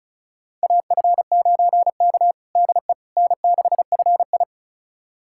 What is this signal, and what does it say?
Morse code, heard as a keyed tone, about 36 wpm, 700 Hz. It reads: AF9K DE N6FI
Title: USA Callsign DE Callsign - 4 Characters - 35wpm